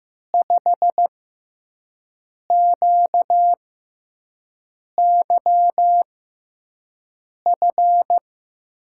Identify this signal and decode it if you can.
Morse code, heard as a keyed tone, about 15 wpm, spaced as for 5 wpm, 700 Hz. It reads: 5QYF
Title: USA Call Signs - 6 Characters - 6x Character Spacing - 15wpm